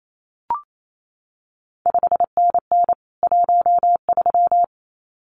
Morse code, heard as a keyed tone, about 28 wpm, 700 Hz. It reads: 5NN 13
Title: CQ WW DX Contest Exchange Encore - 28wpm